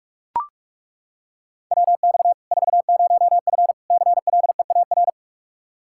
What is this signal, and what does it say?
Morse code, heard as a keyed tone, about 41 wpm, 700 Hz. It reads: WX 40F CLEAR
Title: Practice QSO - Exchange 2 - Info - 45wpm